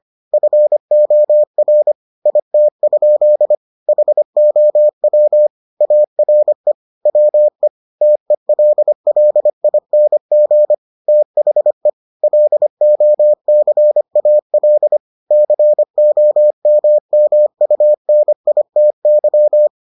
Morse code, heard as a keyed tone, about 25 wpm, 600 Hz. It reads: FOR IT? HOW ARE WE TELLING THE LOCAL COMMUNITY